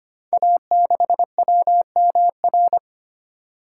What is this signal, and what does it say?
Morse code, heard as a keyed tone, about 25 wpm, 700 Hz. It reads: A6WMR
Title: Code Groups - 5 Letter:Numbers - 25wpm